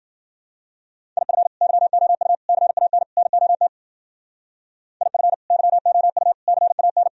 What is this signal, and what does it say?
Morse code, heard as a keyed tone, about 52 wpm, 700 Hz. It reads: IF YOU CAN NOT IF YOU CAN